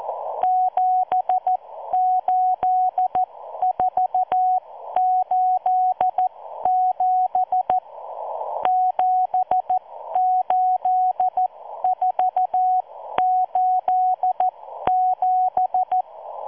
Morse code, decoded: 78487 78487